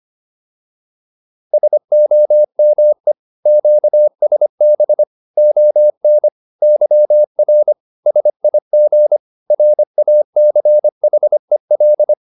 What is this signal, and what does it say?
Morse code, heard as a keyed tone, about 25 wpm, 600 Hz. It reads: SOME QSB ON YR SIG RACHEL